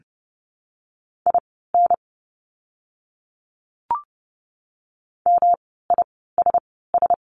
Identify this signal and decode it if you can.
Morse code, heard as a keyed tone, about 30 wpm, spaced as for 17 wpm, 700 Hz. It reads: IN MISS
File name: course / 30:10wpm / 108 Lesson 18f - Letter m - Words 30wpm